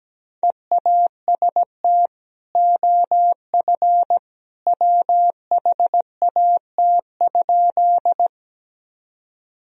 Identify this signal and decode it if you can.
Morse code, heard as a keyed tone, about 17 wpm, 700 Hz. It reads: EAST OF WHAT?
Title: Sentences from Top 1000 Words - Review Part 3 - 17wpm